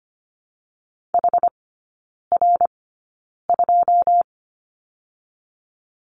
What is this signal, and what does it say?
Morse code, heard as a keyed tone, about 25 wpm, spaced as for 9 wpm, 690 Hz. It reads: HR2
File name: Lesson 14e - Number 2 - Three Characters 25wpm